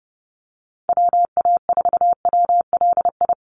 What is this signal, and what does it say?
Morse code, heard as a keyed tone, about 30 wpm, 700 Hz. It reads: WA4WLI
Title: Call Signs - 30wpm